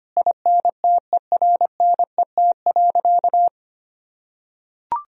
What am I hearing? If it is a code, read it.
Morse code, heard as a keyed tone, about 25 wpm, 700 Hz. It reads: INTERNET.
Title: Sentences from Top 1000 Words - Review Part 4 - 25wpm